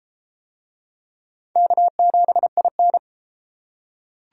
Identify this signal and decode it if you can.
Morse code, heard as a keyed tone, about 31 wpm, 700 Hz. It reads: K7IN